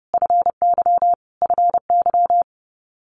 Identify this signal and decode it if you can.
Morse code, heard as a keyed tone, about 30 wpm, 700 Hz. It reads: FY FY
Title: Multi-letter Phonemes Repeated 3x 30wpm